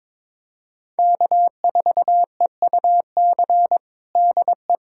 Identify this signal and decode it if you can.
Morse code, heard as a keyed tone, about 22 wpm, 700 Hz. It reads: K4EUC DE
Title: USA Callsign DE Callsign - 5 Characters - 22wpm